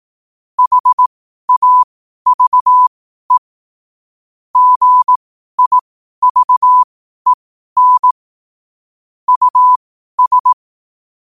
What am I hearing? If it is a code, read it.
Morse code, heard as a keyed tone, about 18 wpm, spaced as for 12 wpm, 1000 Hz. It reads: HAVE GIVEN US
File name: qotdfiles / 12wpm